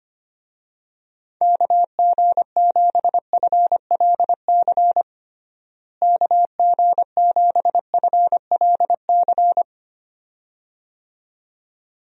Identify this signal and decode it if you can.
Morse code, heard as a keyed tone, about 25 wpm, 700 Hz. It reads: KG7FLC KG7FLC
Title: USA Call Signs - 6 Characters - Repeated 2x - 25wpm